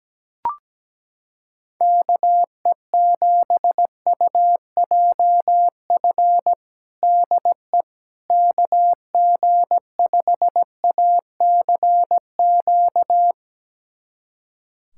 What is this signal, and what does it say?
Morse code, heard as a keyed tone, about 17 wpm, 700 Hz. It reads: KE7UJF DE KG5ACQ
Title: USA Callsign DE Callsign - 6 Characters - 17wpm